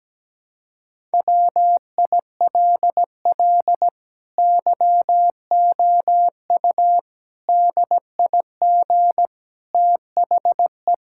Morse code, heard as a keyed tone, about 17 wpm, 700 Hz. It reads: WILL YOU DIG THE
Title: Sentences from Top 1600 Words 17wpm